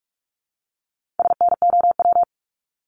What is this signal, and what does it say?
Morse code, heard as a keyed tone, about 46 wpm, 700 Hz. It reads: SNOW
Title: Multi-letter Phonemes Repeated 3x 45wpm